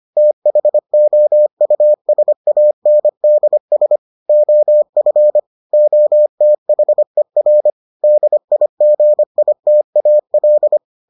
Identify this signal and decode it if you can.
Morse code, heard as a keyed tone, about 25 wpm, 600 Hz. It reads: THOUSANDS OF OTHER DIGITAL